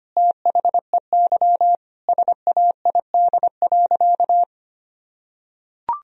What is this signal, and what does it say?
Morse code, heard as a keyed tone, about 25 wpm, 700 Hz. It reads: THEY SAID.
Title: Sentences from Top 1000 Words - Review Part 2 - 25wpm